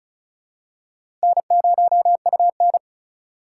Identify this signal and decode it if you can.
Morse code, heard as a keyed tone, about 36 wpm, 700 Hz. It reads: N0UN
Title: USA Call Signs - 4 Characters - 2x Character Spacing - 35wpm